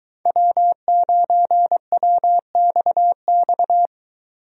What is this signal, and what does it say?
Morse code, heard as a keyed tone, about 22 wpm, 700 Hz. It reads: W9WXX